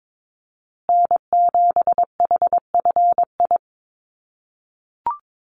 Morse code, heard as a keyed tone, about 22 wpm, 700 Hz. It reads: N7HFI
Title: USA Call Signs Repeated 2x 22wpm